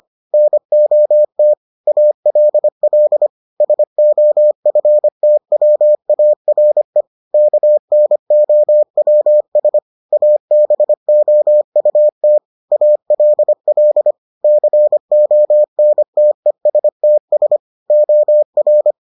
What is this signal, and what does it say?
Morse code, heard as a keyed tone, about 25 wpm, 600 Hz. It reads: NOT ALL SOFTWARE KNOWS ABOUT ALL CONTESTS OR